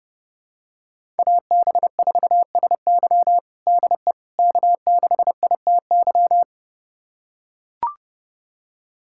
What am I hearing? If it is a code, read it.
Morse code, heard as a keyed tone, about 30 wpm, 700 Hz. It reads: AB4SY DE K6ITY
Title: USA Callsign DE Callsign - 5 Characters - 30wpm